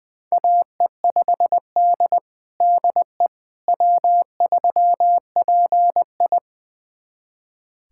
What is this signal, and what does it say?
Morse code, heard as a keyed tone, about 21 wpm, 700 Hz. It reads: AE5D DE W3PI
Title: USA Callsign DE Callsign - 4 Characters - 20wpm